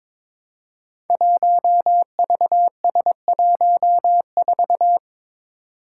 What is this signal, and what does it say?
Morse code, heard as a keyed tone, about 22 wpm, 700 Hz. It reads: 1VS14
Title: Five Challenging Letter-Number Combinations 22wpm